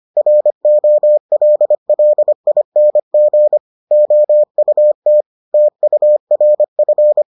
Morse code, heard as a keyed tone, about 25 wpm, 600 Hz. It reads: ROLLING OUT TURF